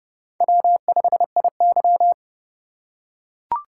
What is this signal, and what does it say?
Morse code, heard as a keyed tone, about 30 wpm, 700 Hz. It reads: W5IY
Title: USA Call Signs - 4 Characters - 3x Character Spacing - 30wpm